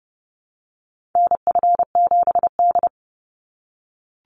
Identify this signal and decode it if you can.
Morse code, heard as a keyed tone, about 30 wpm, 700 Hz. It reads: NF7D
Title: Call Signs - 30wpm